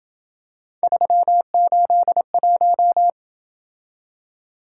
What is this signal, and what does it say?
Morse code, heard as a keyed tone, about 26 wpm, 700 Hz. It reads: 381